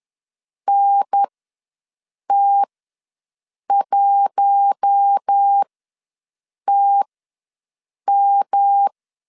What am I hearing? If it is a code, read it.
Morse code, heard as a keyed tone, about 10 wpm, spaced as for 6 wpm, 800 Hz. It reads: NT1TM
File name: LEZIONI / 6 PKX0 / 6 P ws 10wpm 3751